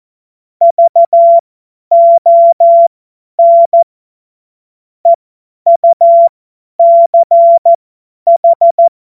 Morse code, heard as a keyed tone, about 14 wpm, spaced as for 10 wpm, 680 Hz. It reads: VON EUCH
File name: Morsekurs / Witze / Witz 05. Oktober 2021, 50 F70